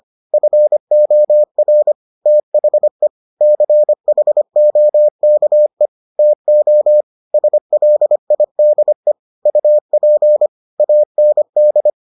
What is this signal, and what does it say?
Morse code, heard as a keyed tone, about 25 wpm, 600 Hz. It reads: FOR THE CHOKE TO SLIDE UP AND